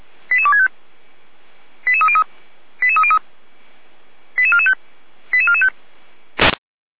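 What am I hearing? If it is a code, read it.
Five-tone selective call (ZVEI), heard as quick runs of five tones, tones of about 70 ms. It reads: 88265 88303 88303 88445 88445
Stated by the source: Five-tone ZVEI selective call 88261, received on SDR.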